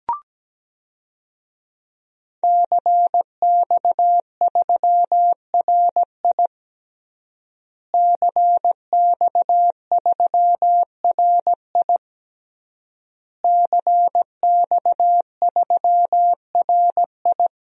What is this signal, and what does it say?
Morse code, heard as a keyed tone, about 17 wpm, 700 Hz. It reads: CX3RI CX3RI CX3RI
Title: DXCC-100 Call Signs - 5 Characters - Repeated 3x - 17wpm